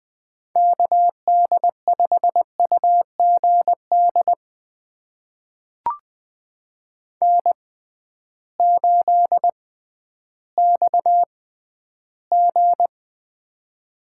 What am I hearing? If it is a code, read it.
Morse code, heard as a keyed tone, about 20 wpm, spaced as for 12 wpm, 700 Hz. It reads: KD5UGD N8XG
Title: USA Call Signs - 6x Character Spacing - 20wpm